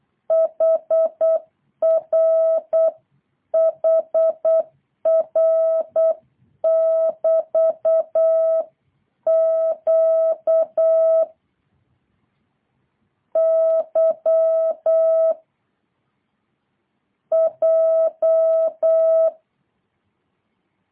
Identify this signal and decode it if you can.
Morse code, heard as a keyed tone, about 8 wpm, 650 Hz. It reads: HR HR <BT> Q Y J